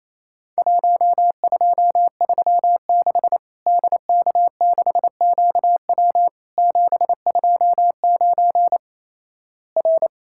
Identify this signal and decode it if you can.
Morse code, heard as a keyed tone, about 28 wpm, 700 Hz. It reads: 1236 DK6QW 729 R
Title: Worked All Europe DX Contest - QTC Exchange - 28wpm